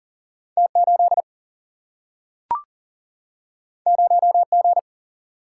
Morse code, heard as a keyed tone, about 40 wpm, 700 Hz. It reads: T 8 0 G
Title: Single-Letter Single-Number 40wpm